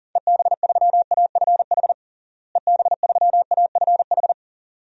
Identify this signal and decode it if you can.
Morse code, heard as a keyed tone, about 40 wpm, 700 Hz. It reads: EB3AFH EB3AFH
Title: DXCC-100 Call Signs - 6 Characters - Repeated 2x - 40wpm